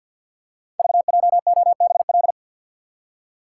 Morse code, heard as a keyed tone, about 53 wpm, 700 Hz. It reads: VJOBL
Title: Code Groups - 5 Letters - 50wpm